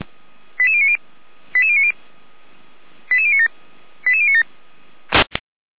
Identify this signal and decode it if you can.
Five-tone selective call (ZVEI), heard as quick runs of five tones, tones of about 80 ms. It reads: 88090 88090 88097 88097